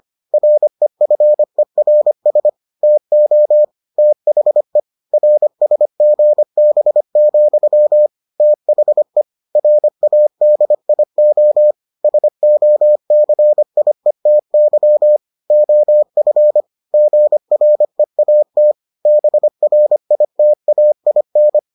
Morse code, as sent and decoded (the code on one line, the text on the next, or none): .-. ..-. .-. ... / - --- / - .... / .-. ... --. -... --..-- / - .... / .-. .- -.. .. --- / ... --- -.-. .. - -.-- / --- ..-. / --. .-. .- - / -... .-. .. - .- .. -.
REFERS TO THE RSGB, THE RADIO SOCIETY OF GREAT BRITAIN